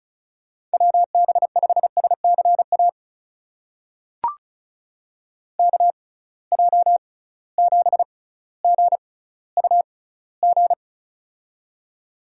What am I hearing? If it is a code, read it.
Morse code, heard as a keyed tone, about 36 wpm, spaced as for 18 wpm, 700 Hz. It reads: WB5SCA KJ7GUG
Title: USA Call Signs - 6 Characters - 6x Character Spacing - 35wpm